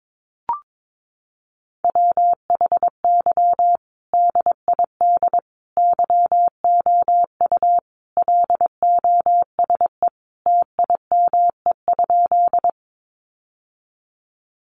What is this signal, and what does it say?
Morse code, heard as a keyed tone, about 22 wpm, 700 Hz. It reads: WHY DID YOU LOSE TIME?
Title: Sentences from Top 1100 Words 22wpm